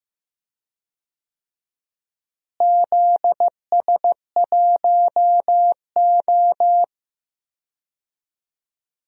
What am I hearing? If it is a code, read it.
Morse code, heard as a keyed tone, about 15 wpm, 700 Hz. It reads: ZS1O
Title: DXCC-100 Call Signs - 4 Characters - Repeated 3x - 15wpm